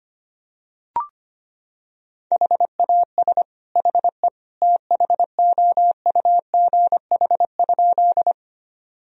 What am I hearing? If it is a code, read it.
Morse code, heard as a keyed tone, about 25 wpm, 700 Hz. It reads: HAS HE THOUGH?
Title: Sentences from Top 1000 Words - Review Part 2 - 25wpm